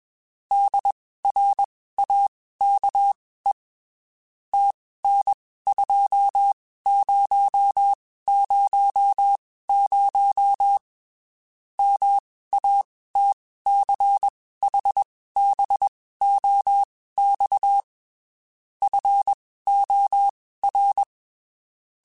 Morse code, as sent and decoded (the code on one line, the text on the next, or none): -.. .-. .- -.- . / - -. ..--- ----- ----- ----- / -- .- - -.-. .... -... --- -..- / ..-. --- .-.
DRAKE TN2000 MATCHBOX FOR